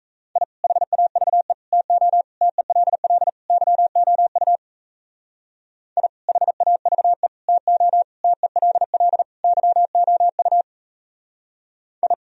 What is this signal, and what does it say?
Morse code, heard as a keyed tone, about 35 wpm, 700 Hz. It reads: I HAVE TO TELL YOU I HAVE TO TELL YOU I